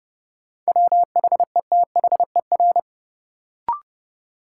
Morse code, heard as a keyed tone, about 31 wpm, 700 Hz. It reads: WHETHER
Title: Sentences from Top 1000 Words - Review Part 3 - New Word Drill - 30wpm